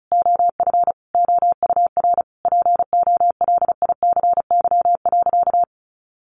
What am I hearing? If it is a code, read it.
Morse code, heard as a keyed tone, about 34 wpm, 700 Hz. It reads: OF OUR POLICY.
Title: Sentences from Top 1200 Words 35wpm